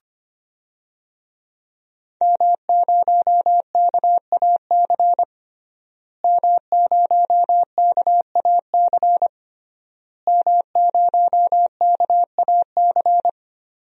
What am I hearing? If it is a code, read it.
Morse code, heard as a keyed tone, about 25 wpm, 700 Hz. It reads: M0KAC M0KAC M0KAC
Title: DXCC-100 Call Signs - 5 Characters - Repeated 3x - 25wpm